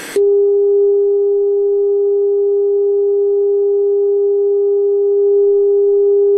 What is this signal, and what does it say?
Dial tone (Japan), 400 Hz.